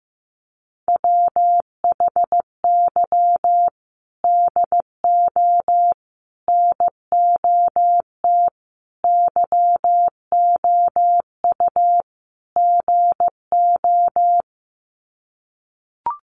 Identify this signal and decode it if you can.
Morse code, heard as a keyed tone, about 15 wpm, 700 Hz. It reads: WHY DO NOT YOU GO
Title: Sets of 5 Words Encore 15wpm